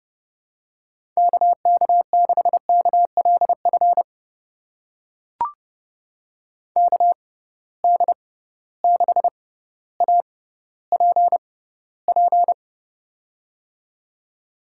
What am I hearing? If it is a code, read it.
Morse code, heard as a keyed tone, about 31 wpm, spaced as for 15 wpm, 700 Hz. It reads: KK6KLF KD6APP
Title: USA Call Signs - 6 Characters - 6x Character Spacing - 30wpm